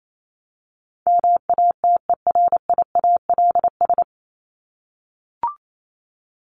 Morse code, heard as a keyed tone, about 28 wpm, 700 Hz. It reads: MATERIALS
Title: ES:S Suffix - Top 100 Words with Base Words 28wpm